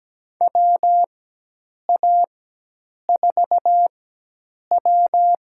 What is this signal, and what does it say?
Morse code, heard as a keyed tone, about 17 wpm, spaced as for 8 wpm, 700 Hz. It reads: WA4W
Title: USA Call Signs - 4x Character Spacing - 17wpm